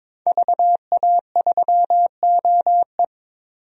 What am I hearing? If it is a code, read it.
Morse code, heard as a keyed tone, about 23 wpm, 700 Hz. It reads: VA3OE
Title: DXCC-100 Call Signs - 5 Characters - 22wpm